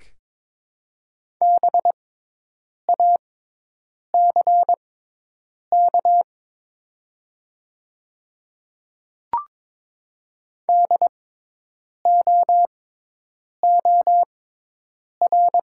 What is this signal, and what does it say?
Morse code, heard as a keyed tone, about 22 wpm, spaced as for 8 wpm, 700 Hz. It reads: BACK DOOR
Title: 6x Character Spacing - Top 500 Words - 22wpm